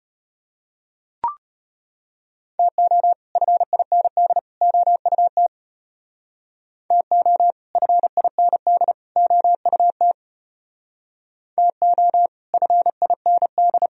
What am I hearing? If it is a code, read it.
Morse code, heard as a keyed tone, about 30 wpm, 700 Hz. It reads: TO FIND OUT TO FIND OUT TO FIND
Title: Speed-Racing - Sets of 3 Words 25wpm